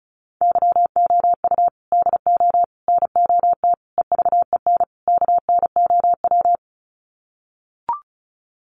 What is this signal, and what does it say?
Morse code, heard as a keyed tone, about 34 wpm, 700 Hz. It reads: YOU DO NOT EVEN KNOW
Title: Sets of 5 Words Encore2 35wpm